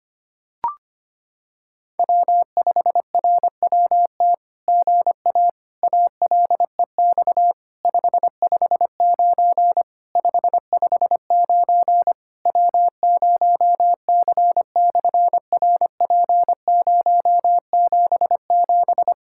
Morse code, heard as a keyed tone, about 26 wpm, 700 Hz. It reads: W5RWT GA ALEX 559 559 W0C/RP077
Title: SOTA QSO Practice 25wpm